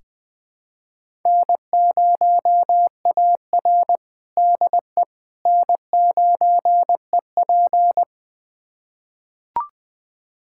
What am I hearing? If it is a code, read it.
Morse code, heard as a keyed tone, about 20 wpm, 700 Hz. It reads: N0AR DE N9EP
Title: USA Callsign DE Callsign - 4 Characters - 20wpm